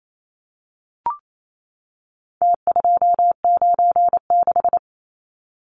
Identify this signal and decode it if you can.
Morse code, heard as a keyed tone, about 28 wpm, 700 Hz. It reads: T296
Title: Numbers - Contest Style - 4-Digit 28wpm